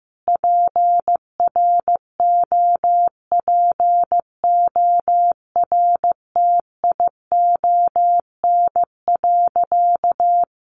Morse code, heard as a keyed tone, about 15 wpm, 700 Hz. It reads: PROPORTION.
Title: A Sound Of Thunder 15wpm